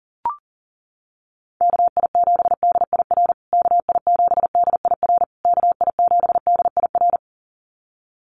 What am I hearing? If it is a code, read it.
Morse code, heard as a keyed tone, about 40 wpm, 700 Hz. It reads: KI7DIR KI7DIR KI7DIR
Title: USA Call Signs Repeated 3x 40wpm